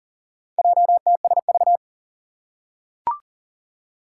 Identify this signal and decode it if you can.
Morse code, heard as a keyed tone, about 39 wpm, 700 Hz. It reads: JTSV
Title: Four Challenging Letter-Number Combinations 40wpm